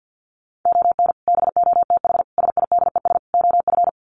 Morse code, heard as a keyed tone, about 50 wpm, 700 Hz. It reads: ON BOTH SIDES OF
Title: Sets of 4 Words 50wpm